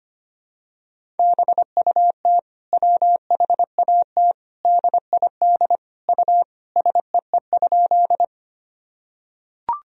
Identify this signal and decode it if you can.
Morse code, heard as a keyed tone, about 25 wpm, 700 Hz. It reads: BUT WHAT DID U SEE?
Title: Sentences from Top 100 Words and Abbreviations 25wpm